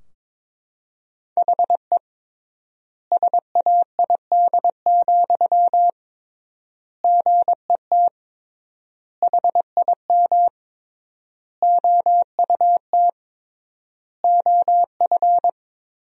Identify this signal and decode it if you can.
Morse code, heard as a keyed tone, about 21 wpm, 700 Hz. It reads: HE SAID, GET HIM OUT OF
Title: 3x Word-Spacing - Sentences from Top 400 Words - 22wpm